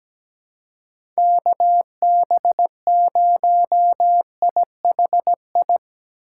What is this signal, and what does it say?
Morse code, heard as a keyed tone, about 17 wpm, 700 Hz. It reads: KB0IHI
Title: USA Call Signs - 2x Character Spacing - 17wpm